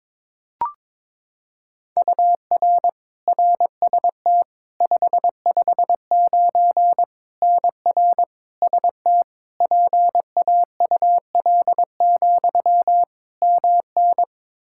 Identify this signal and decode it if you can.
Morse code, heard as a keyed tone, about 22 wpm, 700 Hz. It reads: UR RST 559 NR ST PAUL, MN
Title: Practice QSO - Exchange 1 - Beginning - 22wpm